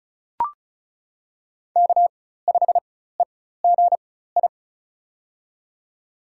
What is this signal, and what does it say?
Morse code, heard as a keyed tone, about 36 wpm, spaced as for 16 wpm, 700 Hz. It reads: K5EGI
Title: USA Call Signs - 4x Character Spacing - 35wpm